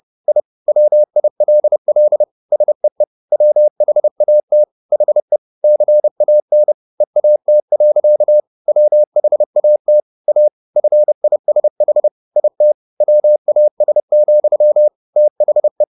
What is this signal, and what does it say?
Morse code, heard as a keyed tone, about 30 wpm, 600 Hz. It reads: I WILL SEE WHAT HE CAN EAT. WHAT A FISH IT WAS, THE